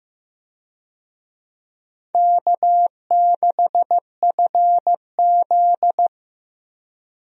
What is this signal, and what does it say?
Morse code, heard as a keyed tone, about 15 wpm, 700 Hz. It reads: K6FZ
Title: USA Call Signs - 4 Characters - Repeated 2x - 15wpm